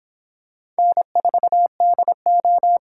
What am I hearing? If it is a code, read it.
Morse code, heard as a keyed tone, about 24 wpm, 700 Hz. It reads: N4DO